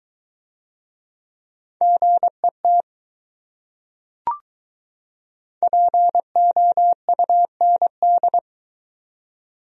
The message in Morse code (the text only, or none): GET POUND